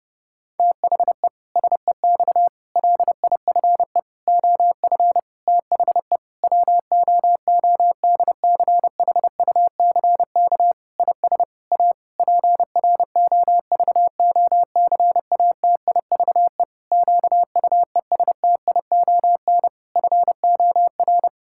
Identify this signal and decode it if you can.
Morse code, heard as a keyed tone, about 30 wpm, 700 Hz. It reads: THE SEX LIFE OF THE WOODCHUCK IS A PROVOCATIVE QUESTION FOR
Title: Pangram Challenge v2 - 30wpm